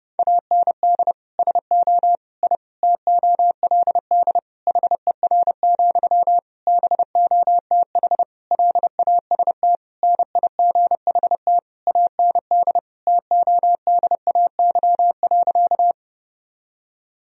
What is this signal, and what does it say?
Morse code, heard as a keyed tone, about 32 wpm, 700 Hz. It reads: AND SO I TOLD HER, BOTH LAST NIGHT AND TODAY.
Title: Sentences from Top 1000 Words - Review Part 4 - 30wpm